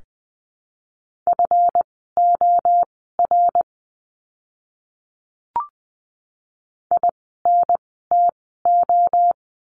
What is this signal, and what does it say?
Morse code, heard as a keyed tone, about 20 wpm, spaced as for 14 wpm, 700 Hz. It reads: FOR INTO
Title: 139 Lesson 23f - Letter f - Words 20wpm